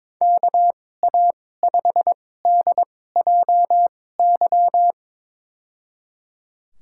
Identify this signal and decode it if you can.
Morse code, heard as a keyed tone, about 22 wpm, spaced as for 16 wpm, 700 Hz. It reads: KA5DJY
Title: USA Call Signs - 6 Characters - 2x Character Spacing - 22wpm